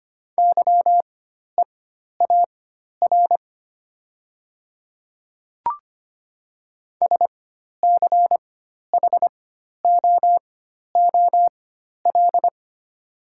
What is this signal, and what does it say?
Morse code, heard as a keyed tone, about 25 wpm, spaced as for 12 wpm, 700 Hz. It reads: YEAR SCHOOL
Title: 4x Character Spacing - Top 300 Words - 25wpm